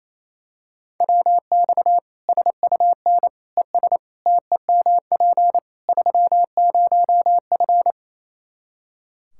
Morse code, heard as a keyed tone, about 28 wpm, 700 Hz. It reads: WX SUN ES TEMP 30F
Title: Practice QSO - Exchange 2 - Info - 28wpm